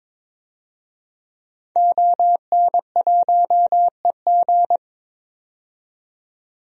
Morse code, heard as a keyed tone, about 22 wpm, 700 Hz. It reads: ON1EG
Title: DXCC-100 Call Signs - 5 Characters - 22wpm